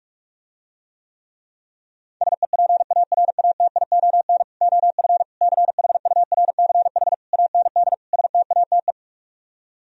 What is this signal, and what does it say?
Morse code, heard as a keyed tone, about 50 wpm, 700 Hz. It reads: SEPARATION OF CHURCH AND STATE